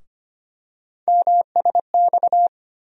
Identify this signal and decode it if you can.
Morse code, heard as a keyed tone, about 25 wpm, 700 Hz. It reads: MSX
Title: USA Call Sign Postfixes - 3 Letters 25wpm